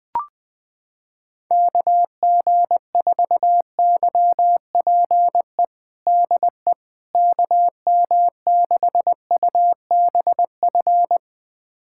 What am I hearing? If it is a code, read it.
Morse code, heard as a keyed tone, about 21 wpm, 700 Hz. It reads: KG4YPE DE KM6UBF
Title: USA Callsign DE Callsign 20wpm